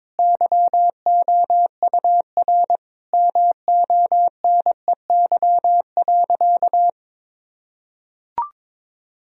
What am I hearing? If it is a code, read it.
Morse code, heard as a keyed tone, about 22 wpm, 700 Hz. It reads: YOUR MONEY.
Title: Sentences from Top 1000 Words - Review Part 2 - 22wpm